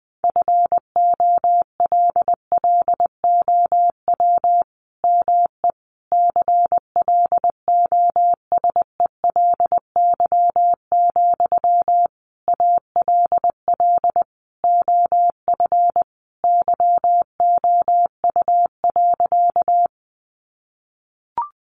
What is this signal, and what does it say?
Morse code, heard as a keyed tone, about 20 wpm, 700 Hz. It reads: FOLLOW ME CLOSELY, ALL OF YOU.
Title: Sentences from Top 1400 Words 20wpm